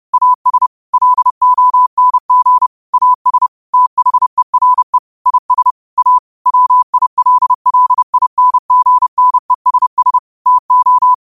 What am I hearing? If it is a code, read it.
Morse code, heard as a keyed tone, about 30 wpm, 1000 Hz. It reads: AS LONG AS THERE IS A WILLINGNESS TO
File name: qotdfiles / 30wpm